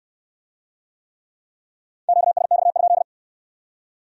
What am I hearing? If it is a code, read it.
Morse code, heard as a keyed tone, about 58 wpm, 700 Hz. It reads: QSX?